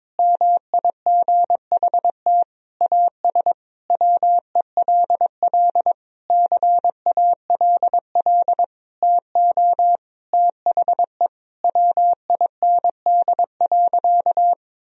Morse code, heard as a keyed tone, about 22 wpm, 700 Hz. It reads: MIGHT AS WELL CALL TO THE WIND.
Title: Sentences from Top 1000 Words - Review Part 2 - 22wpm